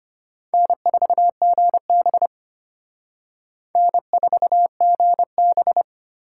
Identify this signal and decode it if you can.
Morse code, heard as a keyed tone, about 31 wpm, 700 Hz. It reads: N4GB N4GB
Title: Speed Racing - USA 1x2 Call Signs 25wpm